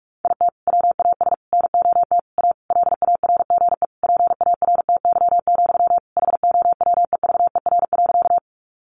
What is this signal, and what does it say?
Morse code, heard as a keyed tone, about 44 wpm, 700 Hz. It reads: IT WAS NOT A LARGE PARTY, HOWEVER.
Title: Sentences from Top 1400 Words 45wpm